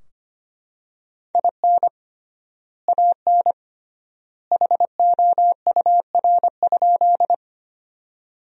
Morse code, heard as a keyed tone, about 25 wpm, 700 Hz. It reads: IN AN HOUR?
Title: 3x Word-Spacing - Sentences from Top 400 Words - 25wpm